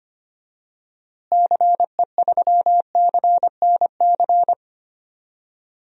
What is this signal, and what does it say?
Morse code, heard as a keyed tone, about 25 wpm, 700 Hz. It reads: CE3CNC
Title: DXCC-100 Call Signs - 6 Characters - Repeated 3x - 25wpm